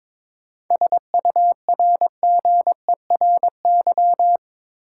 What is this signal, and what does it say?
Morse code, heard as a keyed tone, about 22 wpm, 700 Hz. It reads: SURGERY
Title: Seven Letter Words - 22wpm